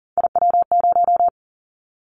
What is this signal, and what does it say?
Morse code, heard as a keyed tone, about 38 wpm, 700 Hz. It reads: IW0
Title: Callsign Prefixes for Top 50 DXCC - 40wpm